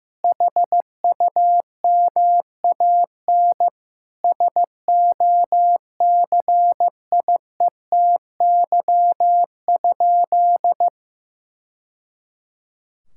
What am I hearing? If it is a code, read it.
Morse code, heard as a keyed tone, about 16 wpm, 700 Hz. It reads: HUMAN SOCIETY?
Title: Sentences from Top 1000 Words - Review Part 6 - 15wpm